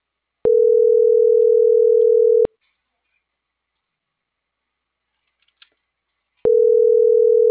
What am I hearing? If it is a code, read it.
Ringback tone (North America), 440 + 480 Hz.